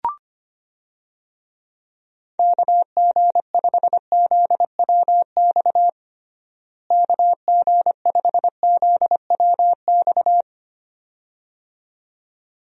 Morse code, heard as a keyed tone, about 25 wpm, 700 Hz. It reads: KG5ZWX KG5ZWX
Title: USA Call Signs - 6 Characters - Repeated 2x - 25wpm